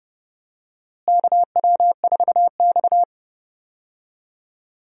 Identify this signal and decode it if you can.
Morse code, heard as a keyed tone, about 32 wpm, 700 Hz. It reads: KW4X
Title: USA Call Signs - 4 Characters - Repeated 2x - 30wpm